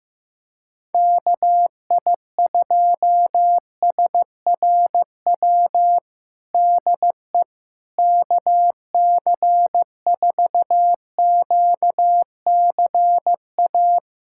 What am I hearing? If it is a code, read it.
Morse code, heard as a keyed tone, about 15 wpm, 700 Hz. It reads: KI2SRW DE KC4QCA
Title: USA Callsign DE Callsign 15wpm